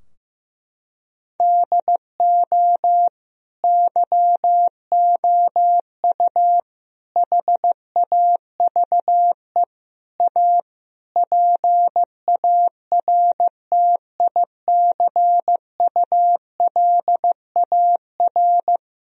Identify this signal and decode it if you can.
Morse code, heard as a keyed tone, about 15 wpm, 700 Hz. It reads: DO YOU HAVE A PARTICULAR
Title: Sentences from Top 1000 Words - Review Part 4 - 15wpm